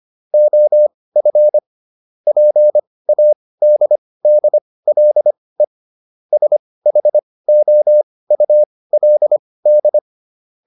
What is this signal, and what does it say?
Morse code, heard as a keyed tone, about 25 wpm, spaced as for 18 wpm, 600 Hz. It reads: OF PADDLE SHOULD